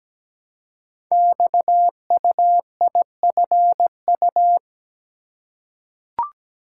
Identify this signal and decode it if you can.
Morse code, heard as a keyed tone, about 17 wpm, 700 Hz. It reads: XUIFU
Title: Code Groups - 5 Letters - 17wpm